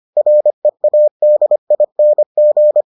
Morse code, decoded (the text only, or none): READING